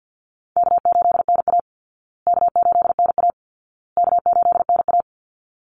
Morse code, heard as a keyed tone, about 50 wpm, 700 Hz. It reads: K8NA K8NA K8NA
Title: Easy Call Signs Repeated 3x - 50wpm